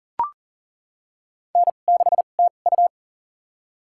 Morse code, heard as a keyed tone, about 38 wpm, spaced as for 28 wpm, 700 Hz. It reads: N6TU
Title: USA Call Signs - 4 Characters - 2x Character Spacing - 40wpm